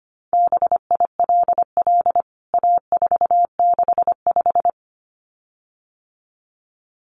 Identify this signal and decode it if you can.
Morse code, heard as a keyed tone, about 25 wpm, 700 Hz. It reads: BILL A465
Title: CWT Exchange 25wpm